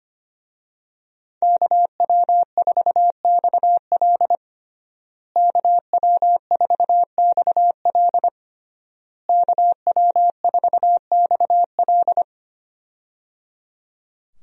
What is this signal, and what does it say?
Morse code, heard as a keyed tone, about 25 wpm, 700 Hz. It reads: KW4XL KW4XL KW4XL
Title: USA Call Signs - 5 Characters - Repeated 3x - 25wpm